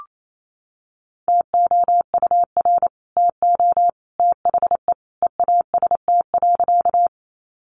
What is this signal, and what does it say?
Morse code, heard as a keyed tone, about 28 wpm, 700 Hz. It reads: TOUR TO THE EAST.
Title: Sentences from Top 1400 Words 28wpm